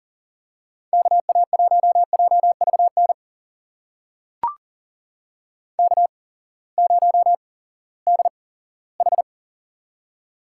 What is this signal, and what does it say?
Morse code, heard as a keyed tone, about 40 wpm, spaced as for 20 wpm, 700 Hz. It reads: KA1JVN K0DH
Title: USA Call Signs - 8x Character Spacing - 40wpm